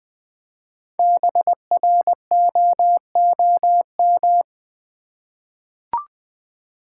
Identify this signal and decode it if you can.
Morse code, heard as a keyed tone, about 20 wpm, 700 Hz. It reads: BROOM
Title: Easy Words with Hints - 20wpm